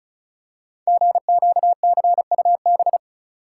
Morse code, heard as a keyed tone, about 34 wpm, 700 Hz. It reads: GQCUB
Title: Code Groups - 5 Letter:Numbers - 35wpm